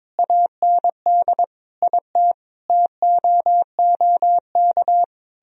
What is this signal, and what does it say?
Morse code, heard as a keyed tone, about 22 wpm, 700 Hz. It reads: AND IT TOOK
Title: Sentences from Top 1000 Words - Review Part 4 - 22wpm